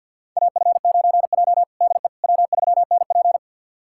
Morse code, heard as a keyed tone, about 53 wpm, 700 Hz. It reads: AF9J DE W3NP
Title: USA Callsign DE Callsign - 4 Characters - 50wpm